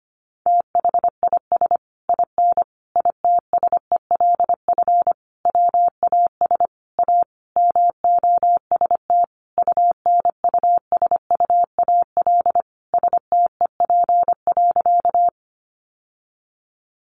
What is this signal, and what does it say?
Morse code, heard as a keyed tone, about 25 wpm, 700 Hz. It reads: THIS IN ITSELF WAS A MOST UNUSUAL STEP.
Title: Sentences from Top 1400 Words 25wpm